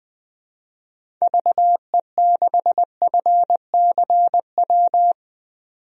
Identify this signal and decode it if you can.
Morse code, heard as a keyed tone, about 21 wpm, 700 Hz. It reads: VE6FCW
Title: DXCC-100 Call Signs - 6 Characters - 20wpm